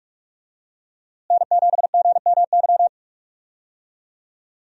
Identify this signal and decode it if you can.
Morse code, heard as a keyed tone, about 44 wpm, 700 Hz. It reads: N7GMY
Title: USA Call Signs - 5 Characters - Repeated 3x - 45wpm